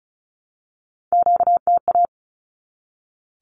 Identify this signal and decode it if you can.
Morse code, heard as a keyed tone, about 34 wpm, 700 Hz. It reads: QTA
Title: Q-Codes 35wpm